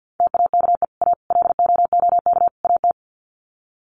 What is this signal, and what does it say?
Morse code, heard as a keyed tone, about 50 wpm, 700 Hz. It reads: TAKE A LOOK AT
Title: Sets of 4 Words 50wpm